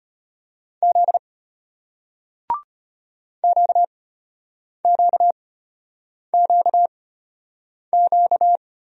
Morse code, heard as a keyed tone, about 33 wpm, spaced as for 8 wpm, 700 Hz. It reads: Z QQQQ